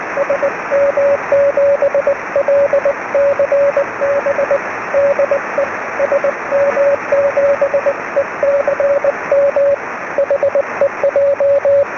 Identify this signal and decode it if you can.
Morse code, heard as a keyed tone, about 20 wpm, 560 Hz, with radio noise. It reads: SM7LCB DE SM7ECM HEJ